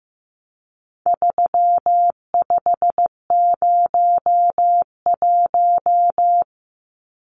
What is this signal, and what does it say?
Morse code, heard as a keyed tone, about 15 wpm, 700 Hz. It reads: 3501
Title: Numbers - 4-Digit 15wpm